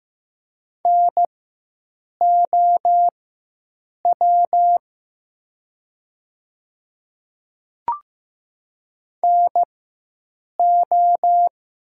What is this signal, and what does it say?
Morse code, heard as a keyed tone, about 15 wpm, spaced as for 7 wpm, 700 Hz. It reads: NOW NO
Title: 4x Character Spacing - Top 100 Words - 15wpm